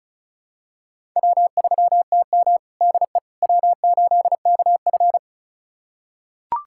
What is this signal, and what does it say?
Morse code, heard as a keyed tone, about 34 wpm, 700 Hz. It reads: W3TM DE W8KF
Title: USA Callsign DE Callsign - 4 Characters - 35wpm